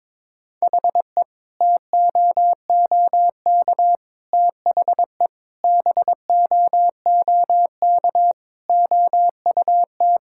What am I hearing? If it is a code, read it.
Morse code, heard as a keyed tone, about 22 wpm, 700 Hz. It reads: HE TOOK THE BOOK OUT
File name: practice / 22wpm / Sentences with 3-Word Phrases - 22wpm